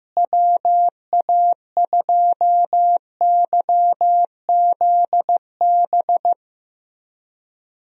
Morse code, heard as a keyed tone, about 16 wpm, 700 Hz. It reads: WA2YZB
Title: USA Call Signs - 6 Characters - Repeated 3x - 15wpm